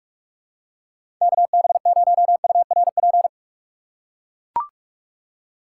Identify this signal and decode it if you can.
Morse code, heard as a keyed tone, about 43 wpm, 700 Hz. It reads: KB0URP
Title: USA Call Signs - 6 Characters - 2x Character Spacing - 45wpm